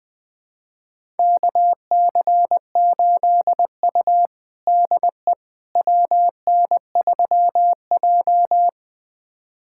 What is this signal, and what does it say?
Morse code, heard as a keyed tone, about 21 wpm, 700 Hz. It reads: KC8U DE WN3J
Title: USA Callsign DE Callsign - 4 Characters - 20wpm